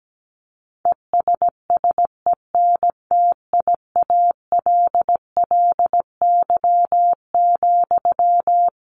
Morse code, heard as a keyed tone, about 17 wpm, 700 Hz. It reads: ESSENTIALLY,
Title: Sentences from Top 1300 Words 17wpm